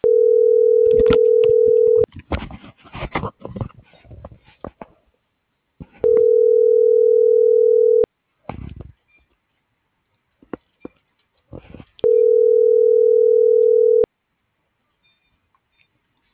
Ringback tone (North America), 440 + 480 Hz.